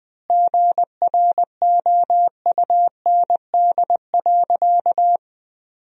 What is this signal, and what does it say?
Morse code, heard as a keyed tone, about 20 wpm, 700 Hz. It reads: GROUND.
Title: Sentences from Top 1000 Words - Review Part 3 - 20wpm